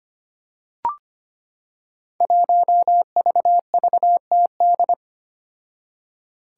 Morse code, heard as a keyed tone, about 25 wpm, 700 Hz. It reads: 1VVTD
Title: Five Challenging Letter-Number Combinations 25wpm